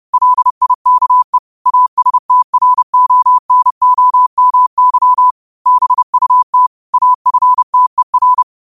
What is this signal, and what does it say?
Morse code, heard as a keyed tone, about 30 wpm, 1000 Hz. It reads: LIKE ASTRONOMY BUT AFTER
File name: qotdfiles / 30wpm